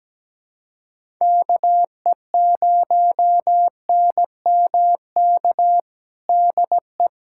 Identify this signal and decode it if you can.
Morse code, heard as a keyed tone, about 17 wpm, 700 Hz. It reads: KE0NMK DE
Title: USA Callsign DE Callsign 17wpm